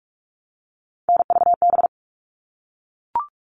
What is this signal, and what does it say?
Morse code, heard as a keyed tone, about 44 wpm, 700 Hz. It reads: NVB
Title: All Cut Numbers - 3-Digit 45wpm